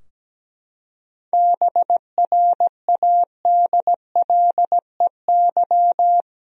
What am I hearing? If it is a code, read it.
Morse code, heard as a keyed tone, about 17 wpm, 700 Hz. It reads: BRADLEY
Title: USA Common Names - 17wpm